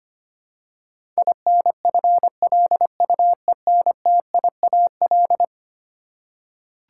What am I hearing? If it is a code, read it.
Morse code, heard as a keyed tone, about 25 wpm, 700 Hz. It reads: INFLUENTIAL
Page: morse-code-ninja-practice-25wpm